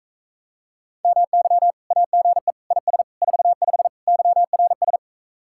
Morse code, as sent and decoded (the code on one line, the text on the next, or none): -- -.-- / .- --. / .. ... / ....- ..... / -.-- .-. ...
MY AGE IS 45 YRS